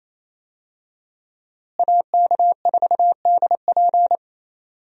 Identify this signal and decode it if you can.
Morse code, heard as a keyed tone, about 28 wpm, 700 Hz. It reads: AK4DP
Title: USA Call Signs - 5 Characters - Repeated 3x - 28wpm